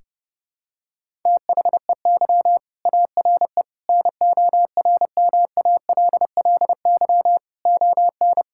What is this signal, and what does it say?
Morse code, heard as a keyed tone, about 30 wpm, 700 Hz. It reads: THEY ARE NORMALLY ON
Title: Sentences from Top 1000 Words - Review Part 6 - 30wpm